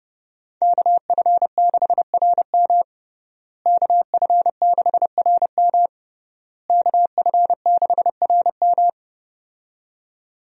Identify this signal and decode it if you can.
Morse code, heard as a keyed tone, about 32 wpm, 700 Hz. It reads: KF6RM KF6RM KF6RM
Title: USA Call Signs - 5 Characters - Repeated 3x - 30wpm